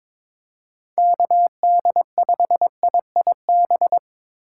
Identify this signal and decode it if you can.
Morse code, heard as a keyed tone, about 25 wpm, 700 Hz. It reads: KD5IIB
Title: Speed Racing - USA 2x3 Call Signs 22wpm